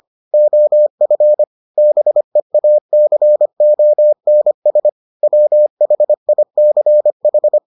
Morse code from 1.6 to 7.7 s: BEACONS WHICH

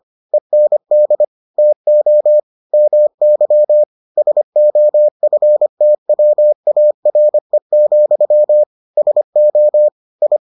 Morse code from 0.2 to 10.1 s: END TO MY SOFTWARE, SO

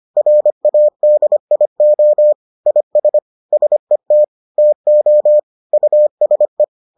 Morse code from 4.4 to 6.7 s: TO USE

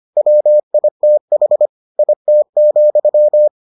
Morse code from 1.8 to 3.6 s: IT,

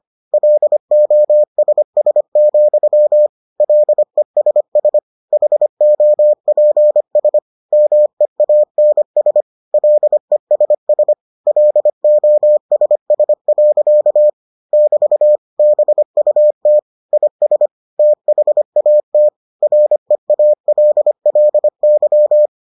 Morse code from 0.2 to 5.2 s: LOSS, LESS

